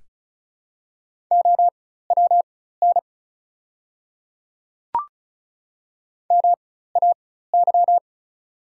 Morse code, heard as a keyed tone, about 36 wpm, spaced as for 16 wpm, 690 Hz. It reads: OWN MAY
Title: 4x Character Spacing - Top 300 Words - 35wpm